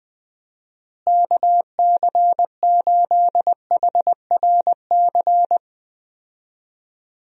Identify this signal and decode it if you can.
Morse code, heard as a keyed tone, about 20 wpm, 700 Hz. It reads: KC8HRC
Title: USA Call Signs - 6 Characters - Repeated 3x - 20wpm